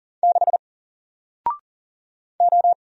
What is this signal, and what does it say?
Morse code, heard as a keyed tone, about 40 wpm, 700 Hz. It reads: 6 O
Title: Single Letter-Number Alt 40wpm